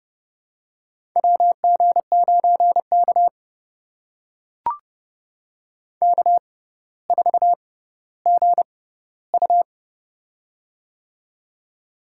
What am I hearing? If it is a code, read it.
Morse code, heard as a keyed tone, about 30 wpm, spaced as for 15 wpm, 700 Hz. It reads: WG9K K4GU
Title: USA Call Signs - 4 Characters - 6x Character Spacing - 30wpm